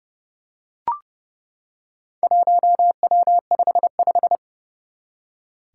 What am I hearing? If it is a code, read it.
Morse code, heard as a keyed tone, about 28 wpm, 700 Hz. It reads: 1W55
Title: Four Challenging Letter-Number Combinations 30wpm